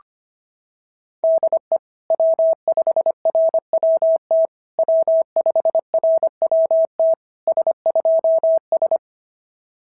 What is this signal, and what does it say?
Morse code, heard as a keyed tone, about 22 wpm, 650 Hz. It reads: DE W5RWT W5RWT S2S
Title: SOTA QSO Practice 25wpm